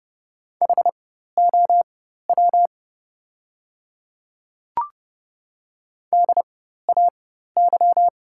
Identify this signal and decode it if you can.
Morse code, heard as a keyed tone, about 31 wpm, spaced as for 14 wpm, 690 Hz. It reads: HOW DAY
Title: 4x Character Spacing - Top 100 Words - 30wpm